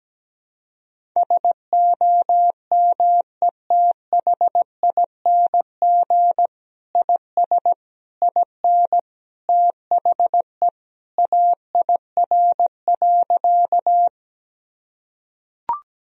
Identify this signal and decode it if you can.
Morse code, heard as a keyed tone, about 17 wpm, 700 Hz. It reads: SOMETHING IS IN THE AIR.
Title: Sentences from Top 1000 Words - Review Part 1 - 17wpm